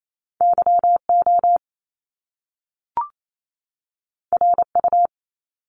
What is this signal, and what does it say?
Morse code, heard as a keyed tone, about 28 wpm, 700 Hz. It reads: YO RU
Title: Top Two Letter Combinations in Words 28wpm